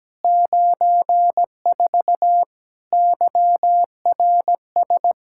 Morse code, heard as a keyed tone, about 17 wpm, 700 Hz. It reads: 94 YRS
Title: Practice QSO - Exchange 3 - More Info - 17wpm